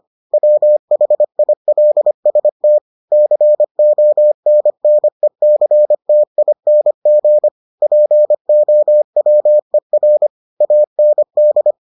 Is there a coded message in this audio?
Yes: Morse code from 0.3 to 11.7 s, heard as a keyed tone, about 25 wpm, 600 Hz. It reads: WHILST CONNECTING POWER AND